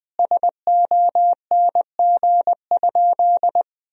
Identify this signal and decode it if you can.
Morse code, heard as a keyed tone, about 21 wpm, 700 Hz. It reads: SONG?
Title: Sentences from Top 1000 Words - Review Part 2 - 20wpm